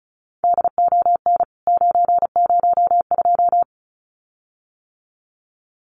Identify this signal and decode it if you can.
Morse code, heard as a keyed tone, about 34 wpm, 700 Hz. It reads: DON 902
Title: CWT Exchange 35wpm